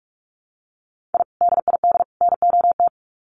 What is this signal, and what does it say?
Morse code, heard as a keyed tone, about 44 wpm, 700 Hz. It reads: I DID NOT
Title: Sets of 3 Words 45wpm